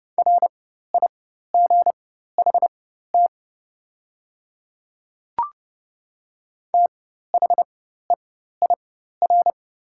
Morse code, heard as a keyed tone, about 30 wpm, spaced as for 14 wpm, 700 Hz. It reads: RIGHT THEIR
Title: 4x Character Spacing - Top 200 Words - 30wpm